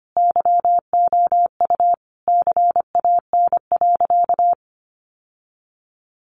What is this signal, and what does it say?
Morse code, heard as a keyed tone, about 25 wpm, 700 Hz. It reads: YOU CAN.
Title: Sentences from Top 1200 Words 25wpm